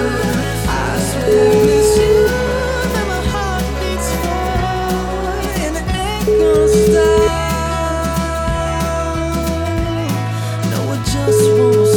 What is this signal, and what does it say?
Ringback tone (Europe, 425 Hz), with noise.